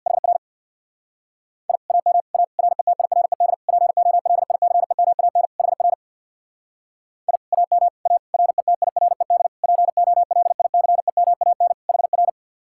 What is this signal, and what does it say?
Morse code, heard as a keyed tone, about 60 wpm, 700 Hz. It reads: HR I AM A RETIRED POLICEMAN HR I AM A RETIRED POLICEMAN HR
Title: Speed-Racing Common QSO Phrases - 50wpm